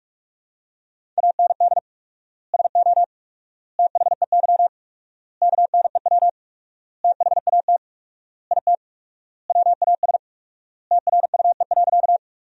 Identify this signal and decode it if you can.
Morse code, heard as a keyed tone, about 41 wpm, 700 Hz. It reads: AND SO THEY KNEW THAT IT WAS TRUE.
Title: 4x Word-Spacing - Sentences from Top 400 Words - 45wpm